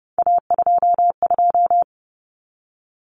Morse code, heard as a keyed tone, about 30 wpm, 700 Hz. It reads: A22
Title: Common Cut Numbers - 3-Digit 30wpm